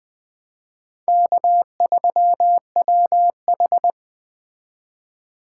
Morse code, heard as a keyed tone, about 20 wpm, 700 Hz. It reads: K3WH
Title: Speed Racing - USA 1x2 Call Signs 17wpm